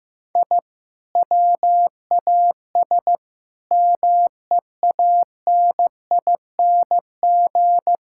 Morse code, heard as a keyed tone, about 15 wpm, 700 Hz. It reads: I WAS MEANING
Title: Sentences from Top 1000 Words - Review Part 6 - 15wpm